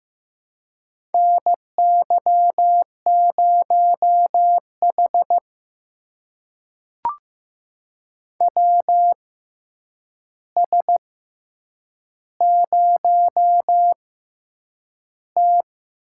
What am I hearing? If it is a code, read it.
Morse code, heard as a keyed tone, about 15 wpm, 700 Hz. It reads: N Y 0 H W S 0 T
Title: USA Call Signs - 4 Characters - 6x Character Spacing - 15wpm